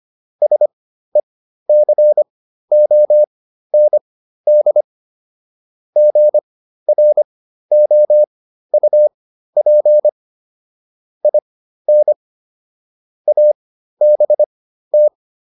Morse code, heard as a keyed tone, about 25 wpm, spaced as for 13 wpm, 600 Hz. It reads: SECOND GROUP IN ABT